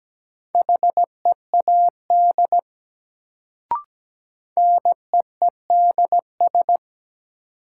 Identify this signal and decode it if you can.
Morse code, heard as a keyed tone, about 17 wpm, 700 Hz. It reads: HEAD NEEDS
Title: Sets of 3 Words - New Word Drill - 17wpm